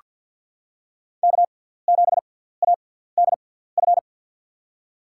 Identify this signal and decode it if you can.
Morse code, heard as a keyed tone, about 50 wpm, spaced as for 17 wpm, 700 Hz. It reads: K7ADF
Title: USA Call Signs - 5 Characters - 6x Character Spacing - 50wpm